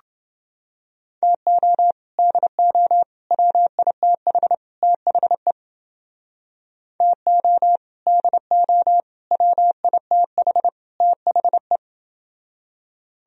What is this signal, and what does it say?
Morse code, heard as a keyed tone, about 29 wpm, 700 Hz. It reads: TO DO WITH THE TO DO WITH THE